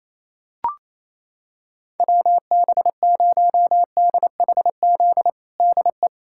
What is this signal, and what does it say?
Morse code, heard as a keyed tone, about 28 wpm, 700 Hz. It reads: WB0DHZ DE
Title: USA Callsign DE Callsign - 6 Characters - 28wpm